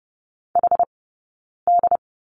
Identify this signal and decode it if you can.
Morse code, heard as a keyed tone, about 30 wpm, 700 Hz. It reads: H D
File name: Lesson 20d - Number 3 - Two Characters 30wpm